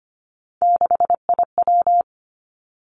Morse code, heard as a keyed tone, about 25 wpm, 700 Hz. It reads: 6IW
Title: Three Challenging Letter-Number Combinations 25wpm